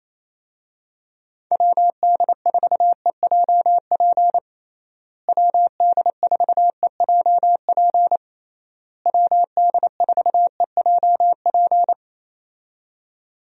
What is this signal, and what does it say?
Morse code, heard as a keyed tone, about 28 wpm, 700 Hz. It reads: WD4EJP WD4EJP WD4EJP
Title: USA Call Signs - 6 Characters - Repeated 3x - 28wpm